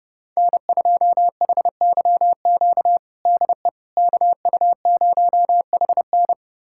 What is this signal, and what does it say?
Morse code, heard as a keyed tone, about 30 wpm, 700 Hz. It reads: N2HYQ DE KU0HN
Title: USA Callsign DE Callsign - 5 Characters - 30wpm